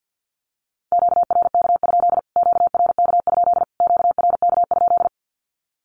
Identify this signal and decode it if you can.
Morse code, heard as a keyed tone, about 48 wpm, 700 Hz. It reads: QRK? QRK? QRK?
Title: Q-Codes Repeated 3x 50wpm